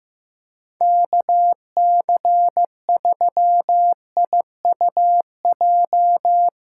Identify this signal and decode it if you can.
Morse code, heard as a keyed tone, about 15 wpm, 700 Hz. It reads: KC3IUJ
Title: USA Call Signs - 6 Characters - 2x Character Spacing - 15wpm